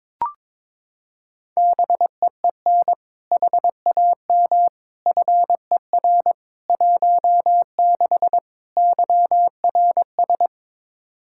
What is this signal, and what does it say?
Morse code, heard as a keyed tone, about 22 wpm, 700 Hz. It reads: BEEN HAM FER 16 YRS
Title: Practice QSO - Exchange 3 - More Info - 22wpm